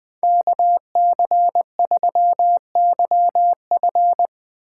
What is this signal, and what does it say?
Morse code, heard as a keyed tone, about 21 wpm, 700 Hz. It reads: KC3YF
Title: USA Call Signs - 3x Character Spacing - 20wpm